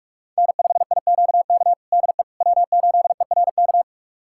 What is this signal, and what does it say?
Morse code, heard as a keyed tone, about 46 wpm, 700 Hz. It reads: N5IQK DE W8ERK
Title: USA Callsign DE Callsign - 5 Characters - 45wpm